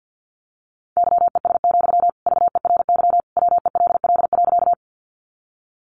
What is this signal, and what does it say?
Morse code, heard as a keyed tone, about 50 wpm, 700 Hz. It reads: YES, VERY WELL.
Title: Sentences from Top 500 Words 50wpm